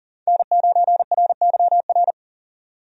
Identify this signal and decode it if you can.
Morse code, heard as a keyed tone, about 40 wpm, 700 Hz. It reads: N9RYR
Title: USA Call Signs - 5 Characters - 40wpm